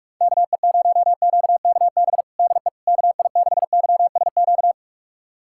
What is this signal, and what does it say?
Morse code, heard as a keyed tone, about 46 wpm, 700 Hz. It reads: KE0QKB DE KI6YSQ
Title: USA Callsign DE Callsign - 6 Characters - 45wpm